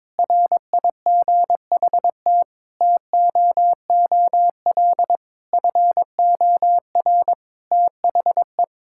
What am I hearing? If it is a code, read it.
Morse code, heard as a keyed tone, about 22 wpm, 700 Hz. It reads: RIGHT TOOL FOR THE
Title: Sentences from Top 1000 Words - Review Part 6 - 22wpm